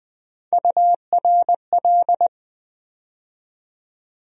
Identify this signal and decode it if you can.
Morse code, heard as a keyed tone, about 20 wpm, 700 Hz. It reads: URL
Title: USA Call Sign Postfixes - 3 Letters 20wpm